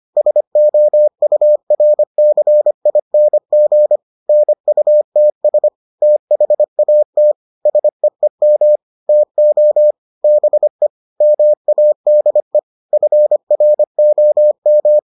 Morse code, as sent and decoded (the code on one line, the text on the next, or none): ... --- ..- .-. -.-. .. -. --. / -. ..- - ... / - .... .- - / ... . -- / - --- / -... / -- .- -.. / ..-. .-. --- --
SOURCING NUTS THAT SEEM TO BE MADE FROM